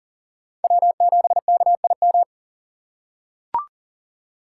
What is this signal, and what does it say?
Morse code, heard as a keyed tone, about 40 wpm, 700 Hz. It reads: W7KIM
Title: USA Call Signs - 5 Characters - 6x Character Spacing - 40wpm